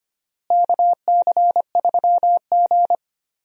Morse code, heard as a keyed tone, about 25 wpm, 700 Hz. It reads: KC3G
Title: USA Call Signs - 4 Characters - Repeated 3x - 25wpm